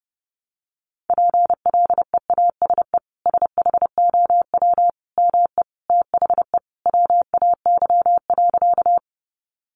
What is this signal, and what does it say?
Morse code, heard as a keyed tone, about 30 wpm, 700 Hz. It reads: PLEASE SHOW ME THE WAY.
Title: Sentences from Top 1500 Words 30wpm